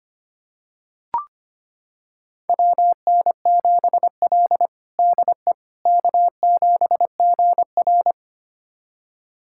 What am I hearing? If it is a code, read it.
Morse code, heard as a keyed tone, about 25 wpm, 700 Hz. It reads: WN7L DE K7GR
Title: USA Callsign DE Callsign - 4 Characters - 25wpm